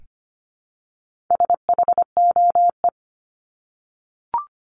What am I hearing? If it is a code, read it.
Morse code, heard as a keyed tone, about 25 wpm, 700 Hz. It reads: SHOE
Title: Top 1751-2000 Words 25wpm